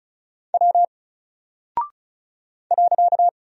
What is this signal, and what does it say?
Morse code, heard as a keyed tone, about 36 wpm, 700 Hz. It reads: W .
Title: Single Character (Letter:Number:Punctuation:Prosign) Alt 35wpm